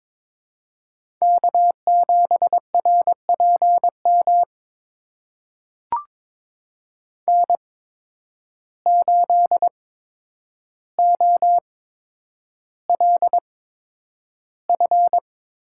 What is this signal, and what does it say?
Morse code, heard as a keyed tone, about 22 wpm, spaced as for 9 wpm, 700 Hz. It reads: K7RPM N8OLF
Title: USA Call Signs - 5 Characters - 8x Character Spacing - 22wpm